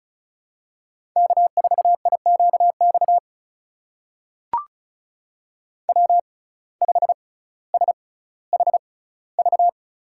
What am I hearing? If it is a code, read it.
Morse code, heard as a keyed tone, about 36 wpm, spaced as for 18 wpm, 700 Hz. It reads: K4IQX W5SHV
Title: USA Call Signs - 6x Character Spacing - 35wpm